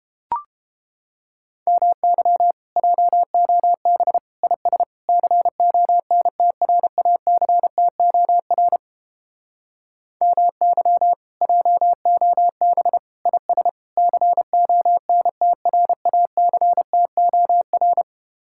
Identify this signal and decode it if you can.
Morse code, heard as a keyed tone, about 31 wpm, 700 Hz. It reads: MY JOB IS CONTRACTOR MY JOB IS CONTRACTOR